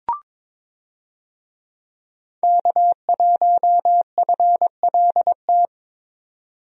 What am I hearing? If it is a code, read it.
Morse code, heard as a keyed tone, about 22 wpm, 700 Hz. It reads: K1FLT
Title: USA Call Signs - 5 Characters - 22wpm